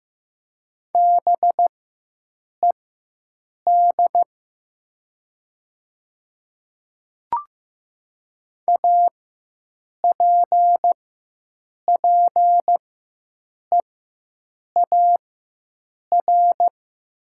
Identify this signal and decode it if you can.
Morse code, heard as a keyed tone, about 15 wpm, spaced as for 7 wpm, 700 Hz. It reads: BED APPEAR
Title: 4x Character Spacing - Top 500 Words - 15wpm